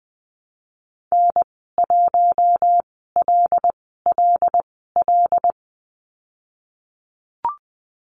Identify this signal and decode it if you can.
Morse code, heard as a keyed tone, about 20 wpm, spaced as for 15 wpm, 700 Hz. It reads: N1LLL
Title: 088 Lesson 15g - Number 5 - Call Signs 20wpm